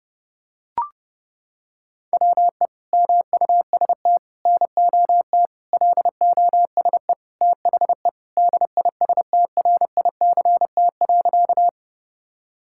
Morse code, heard as a keyed tone, about 30 wpm, 700 Hz. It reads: WE MUST NOT LOSE THE DISTRICT.
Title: Sentences from Top 1600 Words 30wpm